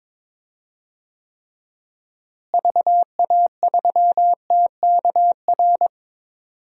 Morse code, heard as a keyed tone, about 22 wpm, 700 Hz. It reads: VA3TKR
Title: DXCC-100 Call Signs - 6 Characters - Repeated 2x - 22wpm